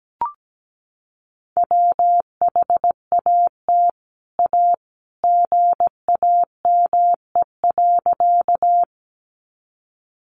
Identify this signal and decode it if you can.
Morse code, heard as a keyed tone, about 17 wpm, 700 Hz. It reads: WHAT A GAME.
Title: Sentences from Top 500 Words 17wpm